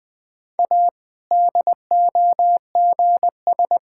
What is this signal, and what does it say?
Morse code, heard as a keyed tone, about 20 wpm, 700 Hz. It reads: A DOGS
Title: Difficult Sentences with Hints - 20wpm